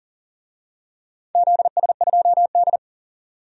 Morse code, heard as a keyed tone, about 38 wpm, 700 Hz. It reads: ZS2D
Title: DXCC-100 Call Signs - 4 Characters - 40wpm